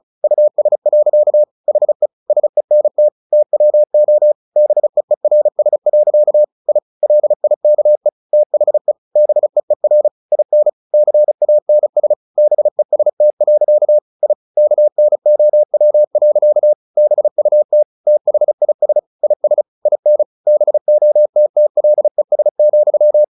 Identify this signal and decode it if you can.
Morse code, heard as a keyed tone, about 34 wpm, 600 Hz. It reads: US. HE SENT TWO BEERS. I LIKE THE BEER IN CANS BEST. I KNOW. BUT THIS IS IN BOTTLES,
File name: the old man and the sea / the old man and the sea 1 35wpm